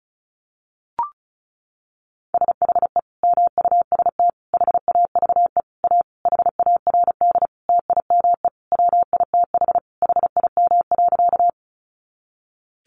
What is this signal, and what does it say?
Morse code, heard as a keyed tone, about 34 wpm, 700 Hz. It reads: SHE MUST HAVE A HARD TIME WITH HIM.
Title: Sentences from Top 300 Words 35wpm